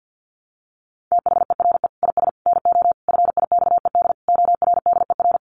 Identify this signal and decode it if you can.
Morse code, heard as a keyed tone, about 50 wpm, 700 Hz. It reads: THERE IS NO FIXED ORDER
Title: Sentences from Top 1100 Words 50wpm